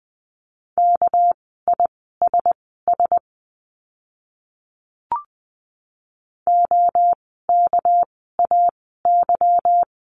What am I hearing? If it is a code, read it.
Morse code, heard as a keyed tone, about 19 wpm, spaced as for 14 wpm, 700 Hz. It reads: KISS OKAY
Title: 199 Lesson 32f - Letter k - Words 20wpm